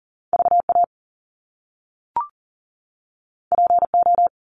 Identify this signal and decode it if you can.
Morse code, heard as a keyed tone, about 39 wpm, 700 Hz. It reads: VA PO
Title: Top Two Letter Combinations in Words 40wpm